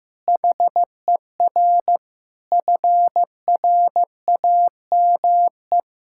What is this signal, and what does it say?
Morse code, heard as a keyed tone, about 15 wpm, 700 Hz. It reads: HER FRAME
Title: Sentences from Top 1000 Words - Review Part 6 - 15wpm